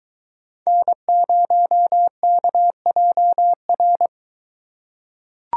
Morse code, heard as a keyed tone, about 22 wpm, 700 Hz. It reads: N0KJR